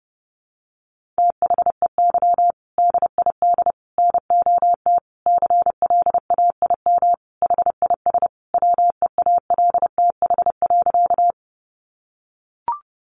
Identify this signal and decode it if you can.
Morse code, heard as a keyed tone, about 28 wpm, 700 Hz. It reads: THEY DID NOT CLAIM HIS WEALTH.
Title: Sentences from Top 1500 Words 30wpm